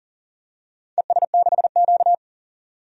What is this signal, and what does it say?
Morse code, heard as a keyed tone, about 40 wpm, 700 Hz. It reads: ES6Q
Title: DXCC-100 Call Signs - 4 Characters - 40wpm